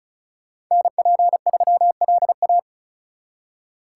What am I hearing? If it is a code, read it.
Morse code, heard as a keyed tone, about 45 wpm, 700 Hz. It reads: NP3LA